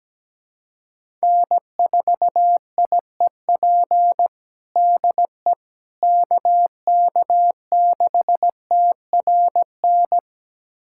Morse code, heard as a keyed tone, about 17 wpm, 700 Hz. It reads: N4IEP DE KK6TRN
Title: USA Callsign DE Callsign 17wpm